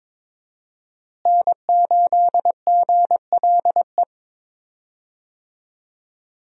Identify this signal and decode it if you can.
Morse code, heard as a keyed tone, about 22 wpm, 700 Hz. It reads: N8GLE
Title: USA Call Signs - 5 Characters - Repeated 2x - 22wpm